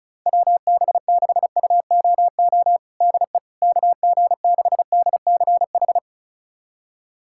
Morse code, heard as a keyed tone, about 36 wpm, 700 Hz. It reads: WB6UOO DE KG6DCH
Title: USA Callsign DE Callsign 35wpm